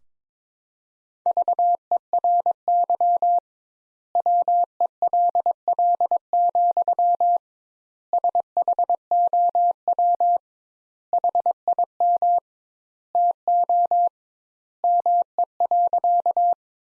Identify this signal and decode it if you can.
Morse code, heard as a keyed tone, about 22 wpm, 700 Hz. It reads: VERY WELL, SHOW HIM TO ME.
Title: Sentences Top 200 Words - 22wpm